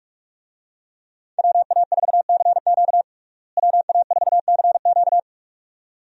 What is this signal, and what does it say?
Morse code, heard as a keyed tone, about 42 wpm, 700 Hz. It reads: WA4CQ WA4CQ
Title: USA Call Signs - 5 Characters - Repeated 2x - 45wpm